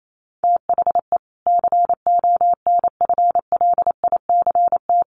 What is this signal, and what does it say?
Morse code, heard as a keyed tone, about 28 wpm, 700 Hz. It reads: THE CONFLICT
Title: Sentences from Top 1300 Words 28wpm